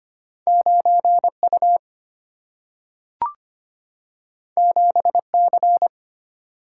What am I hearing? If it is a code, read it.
Morse code, heard as a keyed tone, about 25 wpm, 700 Hz. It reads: 9U 7C
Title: Single-Letter Single-Number 25wpm